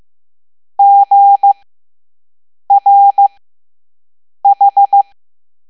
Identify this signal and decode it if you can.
Morse code, heard as a keyed tone, about 15 wpm, 800 Hz. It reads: G R H
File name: Morse / Level7-All-A